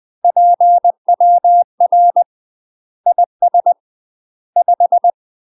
Morse code, heard as a keyed tone, about 20 wpm, 700 Hz. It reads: PWR IS 5
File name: UK MASTER QSOs TRAINING / 20WPM / UK MASTER QSOs 0052